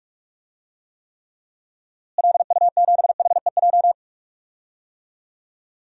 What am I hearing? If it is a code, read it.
Morse code, heard as a keyed tone, about 46 wpm, 700 Hz. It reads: LU7HEJ
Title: DXCC-100 Call Signs - 6 Characters - Repeated 3x - 45wpm